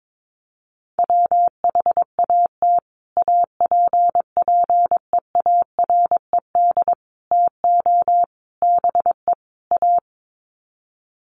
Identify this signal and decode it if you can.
Morse code, heard as a keyed tone, about 22 wpm, 700 Hz. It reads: WHAT APPEARED TO BE A
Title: Sets of 5 Words Encore2 22wpm